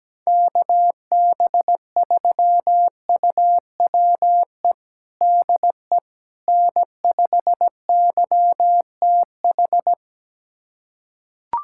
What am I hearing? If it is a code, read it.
Morse code, heard as a keyed tone, about 17 wpm, 700 Hz. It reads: KB3UWE DE N5YTH
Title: USA Callsign DE Callsign 17wpm